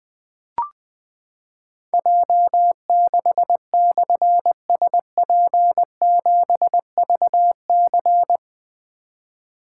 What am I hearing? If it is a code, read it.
Morse code, heard as a keyed tone, about 20 wpm, 700 Hz. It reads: J6/SP7VC
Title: Difficult Call Signs 20wpm